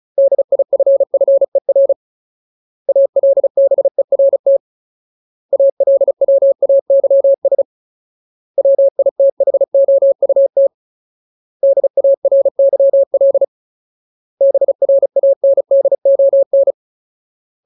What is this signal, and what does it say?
Morse code, heard as a keyed tone, about 34 wpm, 550 Hz. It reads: DIFFER ALBERT ALWAYS WITHOUT DARYL BRANDON
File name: Ultimate Word List / 35WPM Ultimate